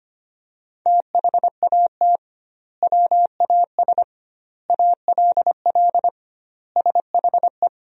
Morse code, heard as a keyed tone, about 25 wpm, 700 Hz. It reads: THAT WAS ALL SHE
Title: 2x Word-Spacing - Sentences from Top 100 Words - 25wpm